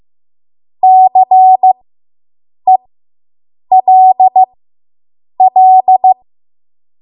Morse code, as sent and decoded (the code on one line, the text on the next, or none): -.-. .-.. .-..
CELL